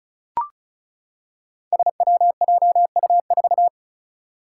Morse code, heard as a keyed tone, about 36 wpm, 700 Hz. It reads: SWJU4
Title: Five Challenging Letter-Number Combinations 35wpm